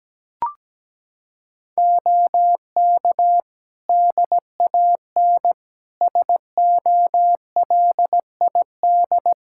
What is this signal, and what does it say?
Morse code, heard as a keyed tone, about 17 wpm, 700 Hz. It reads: OK DAN SOLID
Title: Practice QSO - Exchange 3 - More Info - 17wpm